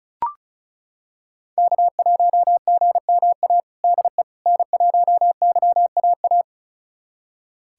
Morse code, heard as a keyed tone, about 36 wpm, 700 Hz. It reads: K1GMA DE N1YAA
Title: USA Callsign DE Callsign - 5 Characters - 35wpm